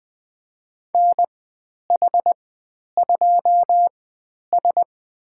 Morse code, heard as a keyed tone, about 20 wpm, spaced as for 10 wpm, 700 Hz. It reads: NH2S
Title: 095 Lesson 16g - Letter c - Call Signs 20wpm